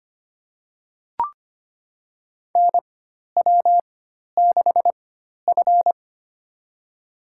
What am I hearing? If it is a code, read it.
Morse code, heard as a keyed tone, about 25 wpm, spaced as for 12 wpm, 700 Hz. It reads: NW6F
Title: USA Call Signs - 4 Characters - 4x Character Spacing - 25wpm